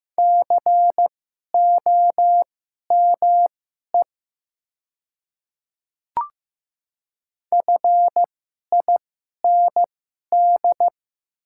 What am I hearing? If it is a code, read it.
Morse code, heard as a keyed tone, about 15 wpm, spaced as for 11 wpm, 700 Hz. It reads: COME FIND
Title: 2x Character Spacing - Top 300 Words - 15wpm